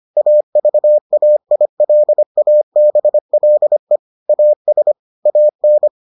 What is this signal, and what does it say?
Morse code, heard as a keyed tone, about 25 wpm, 600 Hz. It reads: AVAILABLE AS AN